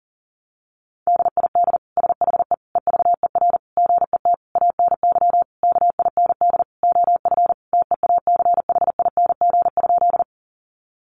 Morse code, heard as a keyed tone, about 37 wpm, 700 Hz. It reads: DID SHE EVER GET ANY KIND OF TEACHING?
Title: Sentences from Top 1100 Words 40wpm